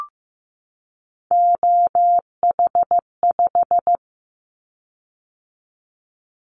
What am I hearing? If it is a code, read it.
Morse code, heard as a keyed tone, about 15 wpm, 700 Hz. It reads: OH5
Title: Callsign Prefixes for Top 50 DXCC - 15wpm